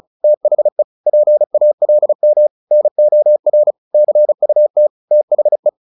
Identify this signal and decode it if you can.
Morse code, heard as a keyed tone, about 34 wpm, 600 Hz. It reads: THE PALM NOR CUT THE
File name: the old man and the sea 2 35wpm